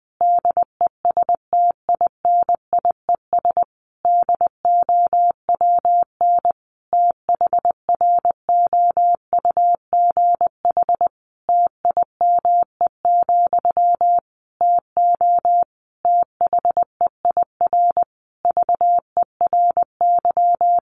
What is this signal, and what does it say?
Morse code, heard as a keyed tone, about 20 wpm, 700 Hz. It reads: DESTINIES DOWN THROUGH TIME, TO THEIR VERY
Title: A Sound Of Thunder 20wpm